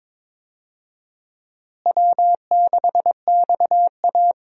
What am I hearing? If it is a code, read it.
Morse code, heard as a keyed tone, about 22 wpm, 700 Hz. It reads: W6XA
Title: USA Call Signs - 4 Characters - Repeated 3x - 22wpm